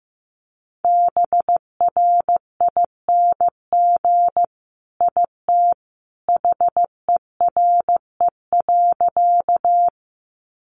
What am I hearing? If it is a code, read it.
Morse code, heard as a keyed tone, about 15 wpm, 700 Hz. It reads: BRING IT HERE.
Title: Sentences from Top 500 Words 15wpm